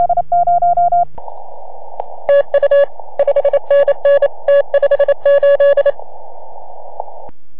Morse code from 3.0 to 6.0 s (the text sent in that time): 5NN T58